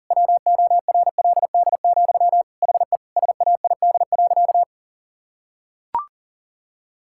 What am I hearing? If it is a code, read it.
Morse code, heard as a keyed tone, about 40 wpm, 700 Hz. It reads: WORLD, HE SAID.
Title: Sentences from Top 1600 Words 40wpm